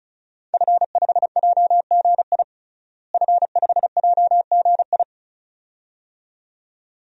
Morse code, heard as a keyed tone, about 36 wpm, 700 Hz. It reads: F5JGI F5JGI
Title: DXCC-100 Call Signs - 5 Characters - Repeated 2x - 35wpm